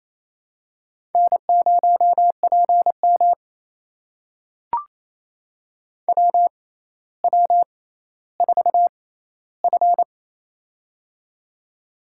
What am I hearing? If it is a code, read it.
Morse code, heard as a keyed tone, about 27 wpm, spaced as for 14 wpm, 700 Hz. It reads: N0PM WW4F
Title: USA Call Signs - 4 Characters - 6x Character Spacing - 28wpm